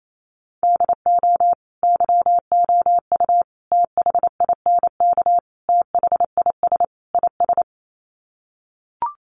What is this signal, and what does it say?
Morse code, heard as a keyed tone, about 28 wpm, 700 Hz. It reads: DO YOU THINK THIS IS
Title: Sets of 5 Words Encore2 28wpm